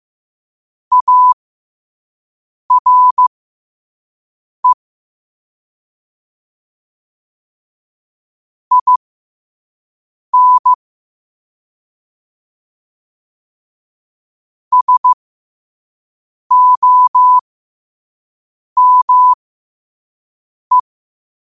Morse code, heard as a keyed tone, about 15 wpm, spaced as for 5 wpm, 1000 Hz. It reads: ARE IN SOME
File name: qotdfiles / 5wpm